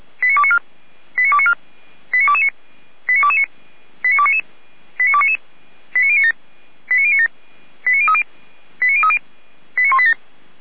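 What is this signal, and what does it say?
Five-tone selective call (ZVEI), heard as quick runs of five tones, tones of about 70 ms. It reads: 89294 89294 89229 89229 89299 89299 89097 89097 89030 89030 89187
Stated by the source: ZVEI five-tone selective call received by SDR, from the audio server of Feuerwehr Orlamünde (fire brigade).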